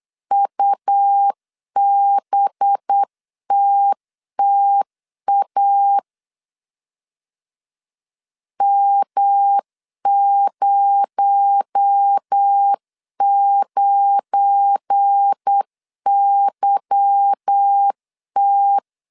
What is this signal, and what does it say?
Morse code, heard as a keyed tone, about 8 wpm, 800 Hz. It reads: UBTTA M09YT
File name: CW / DET / 1 9 1500C S A 8wpm 5637